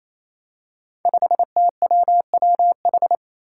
Morse code, heard as a keyed tone, about 27 wpm, 700 Hz. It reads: 5TWWH
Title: Five Challenging Letter-Number Combinations 28wpm